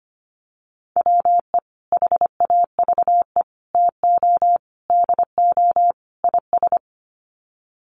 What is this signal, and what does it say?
Morse code, heard as a keyed tone, about 25 wpm, 700 Hz. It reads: WE HAVE TO DO IS
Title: Sets of 5 Words Encore 25wpm